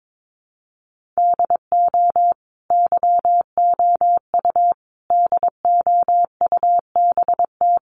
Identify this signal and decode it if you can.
Morse code, heard as a keyed tone, about 22 wpm, 700 Hz. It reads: DO YOU DOUBT
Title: Sentences from Top 1500 Words 22wpm